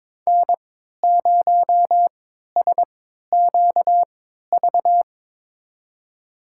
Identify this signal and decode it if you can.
Morse code, heard as a keyed tone, about 22 wpm, spaced as for 13 wpm, 700 Hz. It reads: N0SQV
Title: USA Call Signs - 5 Characters - 3x Character Spacing - 22wpm